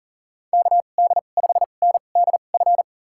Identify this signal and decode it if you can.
Morse code, heard as a keyed tone, about 39 wpm, spaced as for 29 wpm, 690 Hz. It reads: KD5NDF
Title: USA Call Signs - 2x Character Spacing - 40wpm